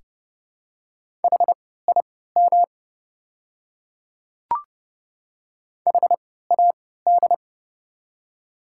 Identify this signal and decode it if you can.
Morse code, heard as a keyed tone, about 30 wpm, spaced as for 17 wpm, 690 Hz. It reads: HIM HAD
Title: 3x Character Spacing - Top 100 Words - 30wpm